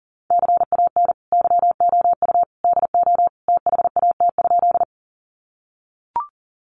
Morse code, heard as a keyed tone, about 40 wpm, 700 Hz. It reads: CAN YOU DO THAT?
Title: Sentences from Top 100 Words 40wpm